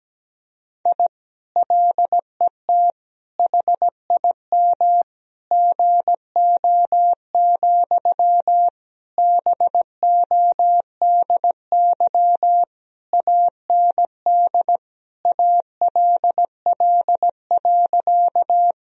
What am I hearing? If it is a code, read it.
Morse code, heard as a keyed tone, about 17 wpm, 700 Hz. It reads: I LET HIM GO, BODY AND ALL.
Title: Sentences from Top 1000 Words - Review Part 2 - 17wpm